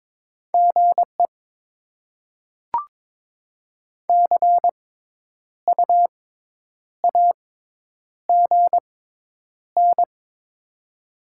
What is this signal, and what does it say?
Morse code, heard as a keyed tone, about 22 wpm, spaced as for 9 wpm, 700 Hz. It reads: GE CU AGN
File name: Top 20 QSO Elements - 6x Character Spacing - 22wpm